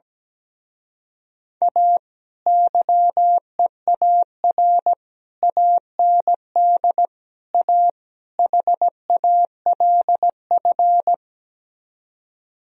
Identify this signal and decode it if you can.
Morse code, heard as a keyed tone, about 17 wpm, 700 Hz. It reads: A YEAR AND A HALF
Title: Speed-Racing - Sets of 5 Words 17wpm